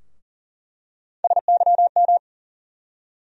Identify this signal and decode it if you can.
Morse code, heard as a keyed tone, about 40 wpm, 700 Hz. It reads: SYM
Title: USA Call Sign Postfixes - 3 Letters 40wpm